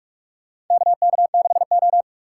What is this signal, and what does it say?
Morse code, heard as a keyed tone, about 44 wpm, 690 Hz. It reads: KK6O
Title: USA Call Signs - 4 Characters - 3x Character Spacing - 45wpm